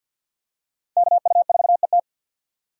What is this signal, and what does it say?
Morse code, heard as a keyed tone, about 52 wpm, 700 Hz. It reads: KU4ET
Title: USA Call Signs - 5 Characters - 6x Character Spacing - 50wpm